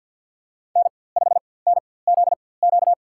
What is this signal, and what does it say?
Morse code, heard as a keyed tone, about 47 wpm, spaced as for 23 wpm, 700 Hz. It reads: N5NZQ
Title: USA Call Signs - 5 Characters - 4x Character Spacing - 50wpm